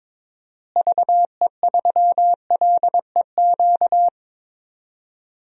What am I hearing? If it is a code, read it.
Morse code, heard as a keyed tone, about 22 wpm, 700 Hz. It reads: VE3LEQ
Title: DXCC-100 Call Signs - 6 Characters - Repeated 3x - 22wpm